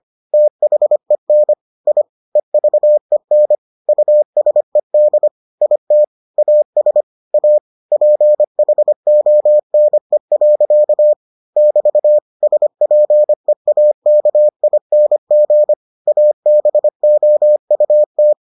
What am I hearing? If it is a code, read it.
Morse code, heard as a keyed tone, about 25 wpm, 600 Hz. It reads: THEN I EVEN USED IT AS A PHONE. <BT> SPEAKING ABOUT